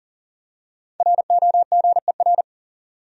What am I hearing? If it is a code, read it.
Morse code, heard as a keyed tone, about 40 wpm, 700 Hz. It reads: ROGER
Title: USA Common Names - Max 5 Letters - 40wpm